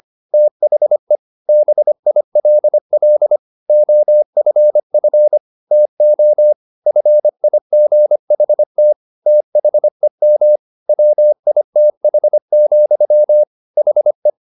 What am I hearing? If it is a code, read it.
Morse code, heard as a keyed tone, about 25 wpm, 600 Hz. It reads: THE BILL OFF TO FIGHT THEM WITH, HE